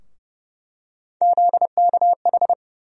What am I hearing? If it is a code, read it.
Morse code, heard as a keyed tone, about 31 wpm, 700 Hz. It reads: ZKH
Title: USA Call Sign Postfixes - 3 Letters 30wpm